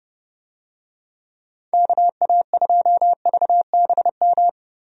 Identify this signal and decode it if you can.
Morse code, heard as a keyed tone, about 30 wpm, 700 Hz. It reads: KA2VBM
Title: USA Call Signs - 6 Characters - Repeated 2x - 30wpm